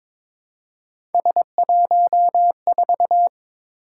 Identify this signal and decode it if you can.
Morse code, heard as a keyed tone, about 22 wpm, 700 Hz. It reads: S14
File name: practice / 22wpm / Three Letter-Number Combinations - 22wpm